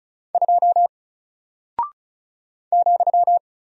Morse code, heard as a keyed tone, about 36 wpm, 700 Hz. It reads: 2 ,
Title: Single Character (Letter:Number:Punctuation:Prosign) Alt 35wpm